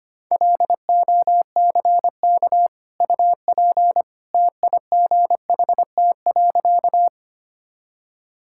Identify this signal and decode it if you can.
Morse code, heard as a keyed tone, about 25 wpm, 700 Hz. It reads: LOCK UP TIGHT.
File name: Easy Sentences with Hints - 25wpm